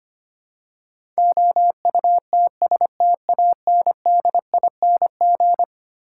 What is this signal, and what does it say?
Morse code, heard as a keyed tone, about 25 wpm, 700 Hz. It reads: OUTSTANDING
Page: morse-code-ninja-practice-25wpm